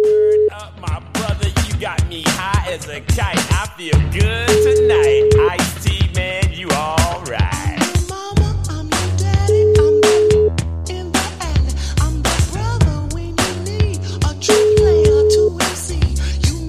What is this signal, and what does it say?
Ringback tone (Europe, 425 Hz).